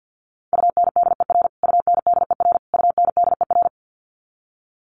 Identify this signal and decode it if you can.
Morse code, heard as a keyed tone, about 50 wpm, 700 Hz. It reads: UNDER UNDER UNDER
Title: Top 200 Words Repeated 50wpm